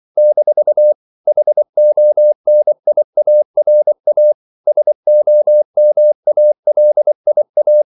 Morse code, heard as a keyed tone, about 24 wpm, 600 Hz. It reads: - HONIARA SOMALIA
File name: CW / Practice / NationalCapitals24